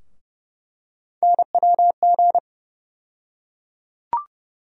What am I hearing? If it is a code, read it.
Morse code, heard as a keyed tone, about 31 wpm, 700 Hz. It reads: NWG
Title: USA Call Sign Postfixes - 3 Letters 30wpm